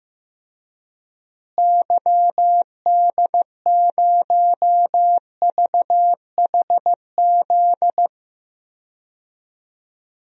Morse code, heard as a keyed tone, about 15 wpm, 700 Hz. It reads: YD0VHZ
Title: DXCC-100 Call Signs - 6 Characters - Repeated 2x - 15wpm